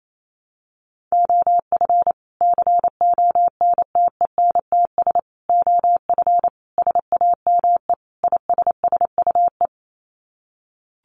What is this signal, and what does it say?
Morse code, heard as a keyed tone, about 28 wpm, 700 Hz. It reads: OF CONTENTS OF SAME ISSUE
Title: Sets of 5 Words Encore2 28wpm